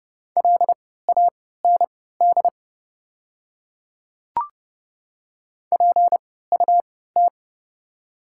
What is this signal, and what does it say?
Morse code, heard as a keyed tone, about 30 wpm, spaced as for 17 wpm, 700 Hz. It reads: LAND PUT
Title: 3x Character Spacing - Top 500 Words - 30wpm